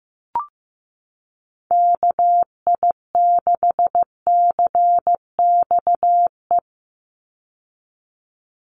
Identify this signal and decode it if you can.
Morse code, heard as a keyed tone, about 15 wpm, 700 Hz. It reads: KI6CXE
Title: USA Call Signs 15wpm